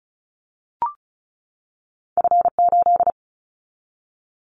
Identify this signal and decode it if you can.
Morse code, heard as a keyed tone, about 36 wpm, 700 Hz. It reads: F8
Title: Callsign Prefixes for Top 25 DXCC - 35wpm